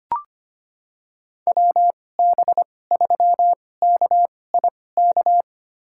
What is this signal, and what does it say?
Morse code, heard as a keyed tone, about 25 wpm, spaced as for 18 wpm, 700 Hz. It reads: WB3KIK
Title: USA Call Signs - 6 Characters - 2x Character Spacing - 25wpm